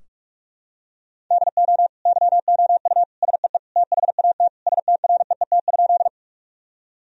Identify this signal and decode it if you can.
Morse code, heard as a keyed tone, about 46 wpm, 700 Hz. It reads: DO YOU SEE THAT STREET?
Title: Sentences from Top 1000 Words - Review Part 3 - 45wpm